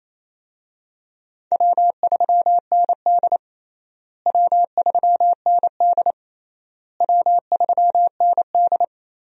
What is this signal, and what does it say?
Morse code, heard as a keyed tone, about 28 wpm, 700 Hz. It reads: W3ND W3ND W3ND
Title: USA Call Signs - 4 Characters - Repeated 3x - 28wpm